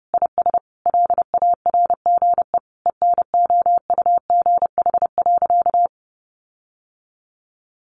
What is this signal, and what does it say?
Morse code, heard as a keyed tone, about 30 wpm, 700 Hz. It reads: IS LARGE ENOUGH.
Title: Sentences from Top 1500 Words 30wpm